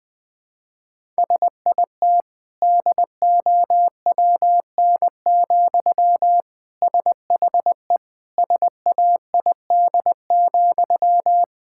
Morse code, heard as a keyed tone, about 20 wpm, 700 Hz. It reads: SIT DOWN, SHE SAID,
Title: Sentences from Top 1000 Words - Review Part 6 - 20wpm